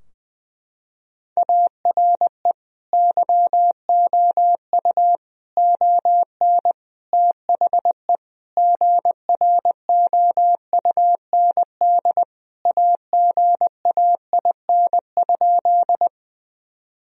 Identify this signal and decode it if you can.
Morse code, heard as a keyed tone, about 20 wpm, 700 Hz. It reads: ARE YOU ON THE GROUND AGAIN?
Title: Sentences from Top 1000 Words - Review Part 2 - 20wpm